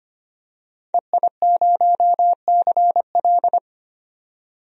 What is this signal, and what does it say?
Morse code, heard as a keyed tone, about 25 wpm, 700 Hz. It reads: EI0CL
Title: DXCC-100 Call Signs - 5 Characters - 25wpm